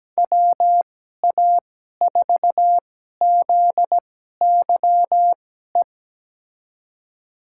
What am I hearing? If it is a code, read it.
Morse code, heard as a keyed tone, about 17 wpm, spaced as for 12 wpm, 700 Hz. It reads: WA4ZYE
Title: USA Call Signs - 2x Character Spacing - 17wpm